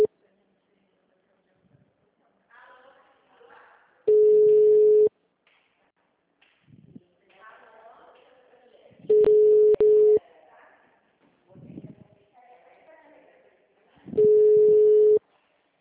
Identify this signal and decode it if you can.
Ringback tone (Europe, 425 Hz), with speech.
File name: out- -2024